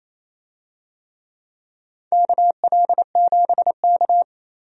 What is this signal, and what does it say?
Morse code, heard as a keyed tone, about 28 wpm, 690 Hz. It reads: KL7K
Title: DXCC-100 Call Signs - 4 Characters - 28wpm